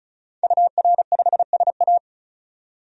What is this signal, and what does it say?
Morse code, heard as a keyed tone, about 34 wpm, 700 Hz. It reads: UR5SA
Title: DXCC-100 Call Signs - 5 Characters - 35wpm